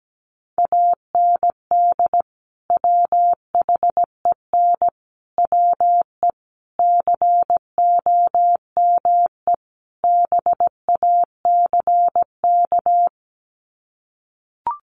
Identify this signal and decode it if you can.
Morse code, heard as a keyed tone, about 17 wpm, 700 Hz. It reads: AND WHEN WE COME BACK
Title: Sets of 5 Words 17wpm